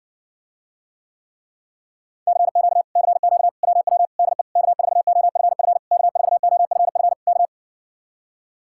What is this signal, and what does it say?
Morse code, heard as a keyed tone, about 65 wpm, 700 Hz. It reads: CQ CQ WW DE K4OFF K4OFF K